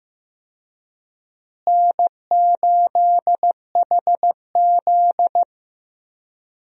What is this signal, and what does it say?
Morse code, heard as a keyed tone, about 15 wpm, 700 Hz. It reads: N8HZ
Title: USA Call Signs - 4 Characters - Repeated 3x - 15wpm